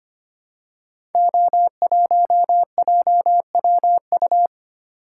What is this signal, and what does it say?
Morse code, heard as a keyed tone, about 25 wpm, 700 Hz. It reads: O1JWU
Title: Code Groups - 5 Letter:Numbers - 25wpm